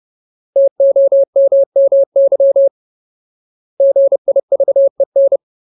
Morse code, heard as a keyed tone, about 30 wpm, 550 Hz. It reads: TOMMY GIVEN
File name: Ultimate Word List / 30WPM Ultimate